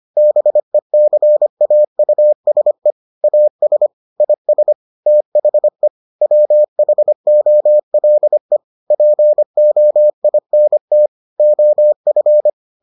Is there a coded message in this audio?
Yes: Morse code from 0.2 to 12.5 s, heard as a keyed tone, about 25 wpm, 600 Hz. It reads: BECAUSE AS IS THE WHOLE POINT OF